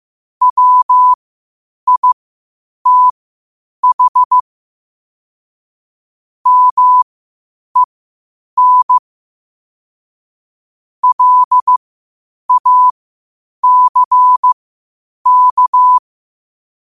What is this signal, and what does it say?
Morse code, heard as a keyed tone, about 15 wpm, spaced as for 8 wpm, 1000 Hz. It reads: WITH MEN LACK
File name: qotdfiles / 8wpm